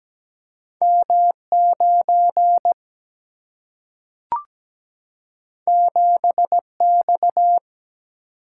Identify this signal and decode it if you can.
Morse code, heard as a keyed tone, about 17 wpm, 700 Hz. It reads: M 9 7 X
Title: Single-Letter Single-Number 17wpm